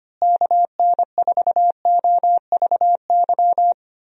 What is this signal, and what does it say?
Morse code, heard as a keyed tone, about 25 wpm, 700 Hz. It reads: KN4OVY
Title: USA Call Signs - 6 Characters - Repeated 3x - 25wpm